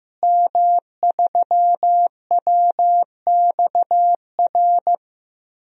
Morse code, heard as a keyed tone, about 15 wpm, 700 Hz. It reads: M3WXR
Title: DXCC-100 Call Signs - 5 Characters - Repeated 3x - 15wpm